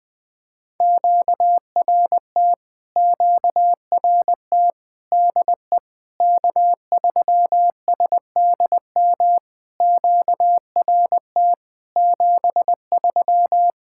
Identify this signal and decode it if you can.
Morse code, heard as a keyed tone, about 20 wpm, 700 Hz. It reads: QRT QRT DE K3SDM QRT 73
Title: POTA QSO Practice - 20wpm